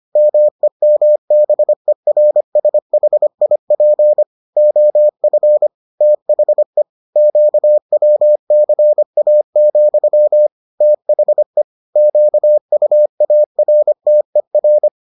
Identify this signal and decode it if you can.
Morse code, heard as a keyed tone, about 25 wpm, 600 Hz. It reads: MEMBERSHIP OF THE QWCA, THE QUARTER